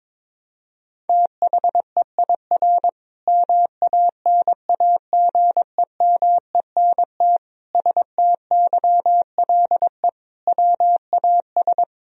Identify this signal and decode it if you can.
Morse code, heard as a keyed tone, about 22 wpm, 700 Hz. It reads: THEIR MANAGEMENT STYLE WAS
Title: Sentences from Top 1000 Words - Review Part 4 - 22wpm